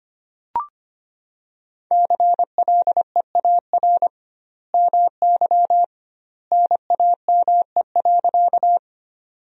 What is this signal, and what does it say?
Morse code, heard as a keyed tone, about 25 wpm, 700 Hz. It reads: CLEAR MY NAME.
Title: 2x Word-Spacing - Sentences from Top 500 Words - 25wpm